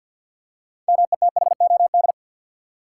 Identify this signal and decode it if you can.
Morse code, heard as a keyed tone, about 53 wpm, 700 Hz. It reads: METHOD
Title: Six Letter Words - 50wpm